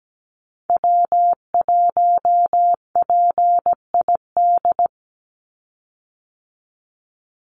Morse code, heard as a keyed tone, about 17 wpm, 700 Hz. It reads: W1PID
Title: Call Signs Repeated 3x - 17wpm